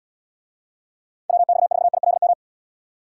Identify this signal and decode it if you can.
Morse code, heard as a keyed tone, about 77 wpm, 700 Hz. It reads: WX4ILM